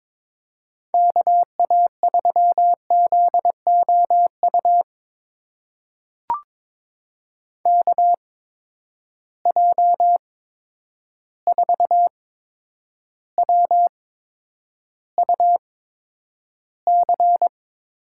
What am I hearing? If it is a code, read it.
Morse code, heard as a keyed tone, about 22 wpm, spaced as for 9 wpm, 700 Hz. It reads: KA3ZOU KJ4WUC
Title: USA Call Signs - 8x Character Spacing - 22wpm